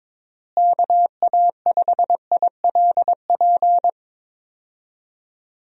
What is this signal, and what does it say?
Morse code, heard as a keyed tone, about 22 wpm, 700 Hz. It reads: KA5ILP
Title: USA Call Signs - 6 Characters - Repeated 2x - 22wpm